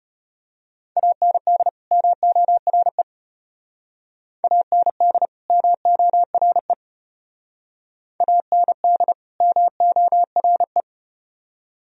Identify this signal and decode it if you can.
Morse code, heard as a keyed tone, about 33 wpm, 700 Hz. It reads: AND MORE AND MORE AND MORE